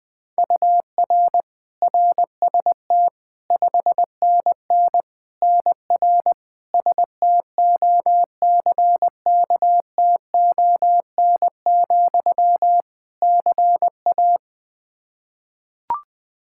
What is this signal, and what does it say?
Morse code, heard as a keyed tone, about 21 wpm, 700 Hz. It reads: UR RST 5NN NR STOCKTON, CA
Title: Practice QSO - Exchange 1 - Beginning - 20wpm